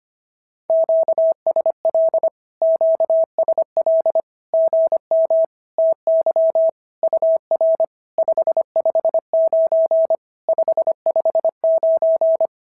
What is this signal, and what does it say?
Morse code, heard as a keyed tone, about 24 wpm, 650 Hz. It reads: QSL QSL GM TY UR 559 559
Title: SOTA QSO Practice 25wpm